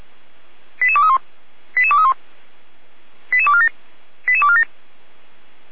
Five-tone selective call (ZVEI), heard as quick runs of five tones, tones of about 70 ms. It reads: 88231 88231 88247 88247
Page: /455/SDR/ZVEI/88213/